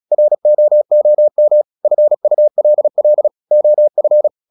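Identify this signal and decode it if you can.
Morse code, heard as a keyed tone, about 36 wpm, 600 Hz. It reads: ROOM FULL OF
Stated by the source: Morse practice at 36 wpm, text of Ken Follett's novel Never, from CW med Gnister.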